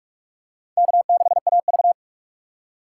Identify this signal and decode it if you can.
Morse code, heard as a keyed tone, about 46 wpm, 700 Hz. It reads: K6AV
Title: USA Call Signs - 2x Character Spacing - 45wpm